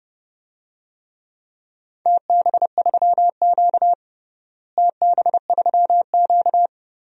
Morse code, heard as a keyed tone, about 32 wpm, 700 Hz. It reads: TB3Q TB3Q
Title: DXCC-100 Call Signs - 4 Characters - Repeated 2x - 30wpm